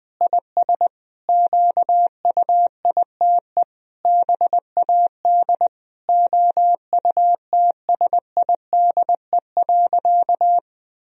Morse code, heard as a keyed tone, about 21 wpm, 700 Hz. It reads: IS QUITE BAD OUTSIDE.
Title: Sentences from Top 1000 Words - Review Part 4 - 20wpm